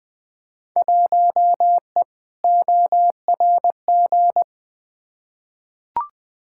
Morse code, heard as a keyed tone, about 20 wpm, 700 Hz. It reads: 1E ORG
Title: Field Day Exchanges - 20wpm